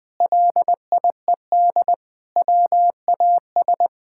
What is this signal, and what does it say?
Morse code, heard as a keyed tone, about 20 wpm, 700 Hz. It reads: LIED WAS
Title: Sentences with 4-Word Phrases - Spoken Beginning - 20wpm